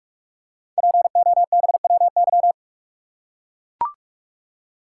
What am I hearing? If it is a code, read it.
Morse code, heard as a keyed tone, about 44 wpm, 690 Hz. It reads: POBWY
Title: Code Groups - 5 Letters - 45wpm